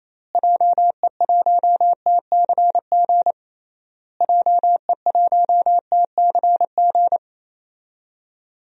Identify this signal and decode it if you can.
Morse code, heard as a keyed tone, about 28 wpm, 700 Hz. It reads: JE1TCG JE1TCG
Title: DXCC-100 Call Signs - 6 Characters - Repeated 3x - 28wpm